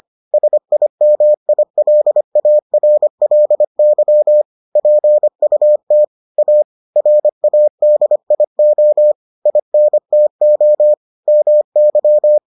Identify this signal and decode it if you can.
Morse code, heard as a keyed tone, about 25 wpm, 600 Hz. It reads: SIMILARLY PUT A RADIO INTO MY